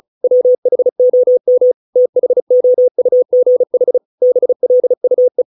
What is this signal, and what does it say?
Morse code, heard as a keyed tone, about 35 wpm, 500 Hz. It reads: WHOM THOUGH BLUE